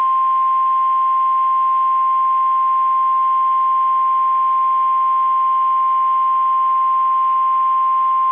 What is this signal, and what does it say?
Emergency Alert System (SAME) tones (the 1050 Hz attention tone). It is heard from 0.5 to 7.2 s.